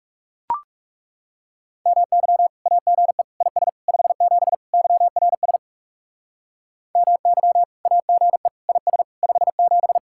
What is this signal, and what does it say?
Morse code, heard as a keyed tone, about 38 wpm, 700 Hz. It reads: MY AGE IS 57 YRS MY AGE IS 57